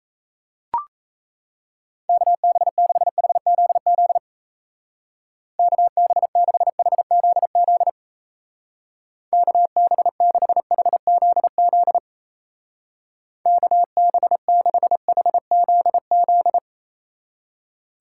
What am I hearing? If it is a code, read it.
Morse code, heard as a keyed tone, about 33 wpm, 700 Hz. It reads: KB6HZZ KB6HZZ KB6HZZ KB6HZZ
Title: Speed Racing - USA 2x3 Call Signs 28wpm